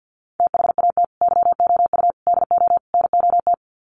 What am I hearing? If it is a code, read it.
Morse code, heard as a keyed tone, about 49 wpm, 700 Hz. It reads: THAT YOU DO NOT
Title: Sets of 4 Words 50wpm